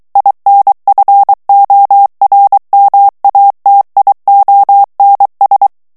Morse code, heard as a keyed tone, about 23 wpm, 800 Hz. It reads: INFORMATIONS
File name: ALLEMAND texte clair 1 vitesse 22 mots minute